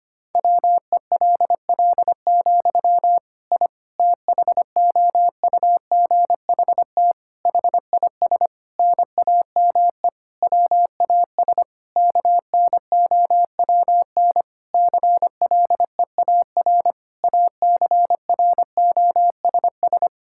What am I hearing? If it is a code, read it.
Morse code, heard as a keyed tone, about 25 wpm, 700 Hz. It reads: WELL, I THOUGHT HIS NAME WAS KNOWN CLEAR ACROSS
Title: Sentences from Top 1000 Words - Review Part 5 - 25wpm